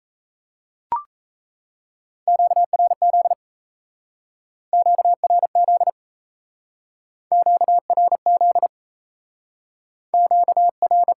Morse code, heard as a keyed tone, about 40 wpm, 700 Hz. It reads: QRZ QRZ QRZ Q R